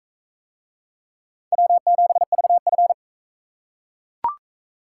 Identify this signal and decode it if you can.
Morse code, heard as a keyed tone, about 40 wpm, 700 Hz. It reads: W7VF